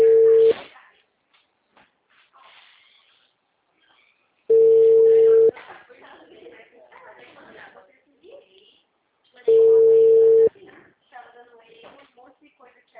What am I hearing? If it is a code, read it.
Ringback tone (China, 450 Hz), with speech.
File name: out- -2002- - - .53848